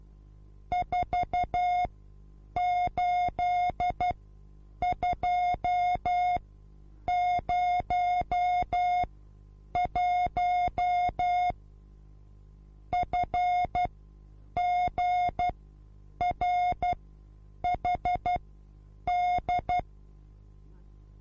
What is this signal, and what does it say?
Morse code, heard as a keyed tone, about 12 wpm, spaced as for 8 wpm, 720 Hz. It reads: 48201 FGRHD